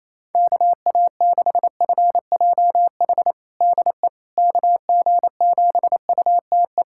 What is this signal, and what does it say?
Morse code, heard as a keyed tone, about 28 wpm, 700 Hz. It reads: KA6FJH DE KG7UTE
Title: USA Callsign DE Callsign - 6 Characters - 28wpm